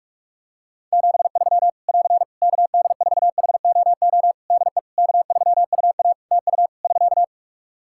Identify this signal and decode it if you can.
Morse code, heard as a keyed tone, about 46 wpm, 700 Hz. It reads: 73 <AR> KD4HOO DE K3UA TU <SK>
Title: Practice QSO - Exchange 4 - Ending - 45wpm